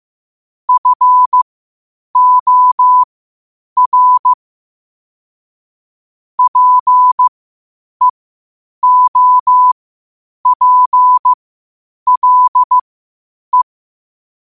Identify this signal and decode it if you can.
Morse code, heard as a keyed tone, about 15 wpm, spaced as for 8 wpm, 1000 Hz. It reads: FOR PEOPLE
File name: qotdfiles / 8wpm